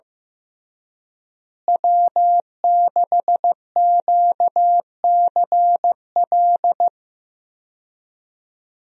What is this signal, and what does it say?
Morse code, heard as a keyed tone, about 15 wpm, 700 Hz. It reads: W6QCL
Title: USA Call Signs - 5 Characters - Repeated 3x - 15wpm